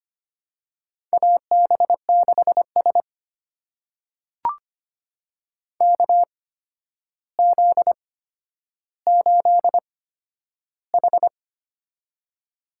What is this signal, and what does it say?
Morse code, heard as a keyed tone, about 25 wpm, spaced as for 11 wpm, 700 Hz. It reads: AB6S KZ8H
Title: USA Call Signs - 4 Characters - 8x Character Spacing - 25wpm